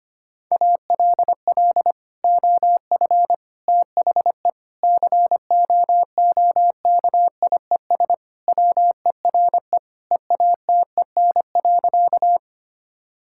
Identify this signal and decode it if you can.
Morse code, heard as a keyed tone, about 25 wpm, 700 Hz. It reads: ALL OF THE COOKIES WERE EATEN.
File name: practice / 25wpm / Sentences with 2-Word Phrases - 25wpm